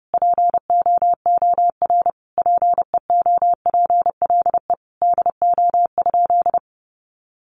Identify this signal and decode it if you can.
Morse code, heard as a keyed tone, about 30 wpm, 700 Hz. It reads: POOR PEOPLE DO?
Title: Sentences from Top 1100 Words 30wpm